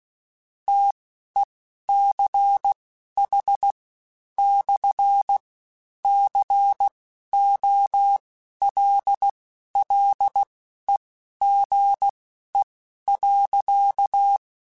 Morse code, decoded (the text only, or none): TECH/COLLEGE.